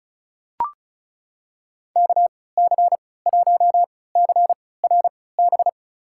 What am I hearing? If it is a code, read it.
Morse code, heard as a keyed tone, about 34 wpm, spaced as for 20 wpm, 700 Hz. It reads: KC1CRB
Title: USA Call Signs - 6 Characters - 3x Character Spacing - 35wpm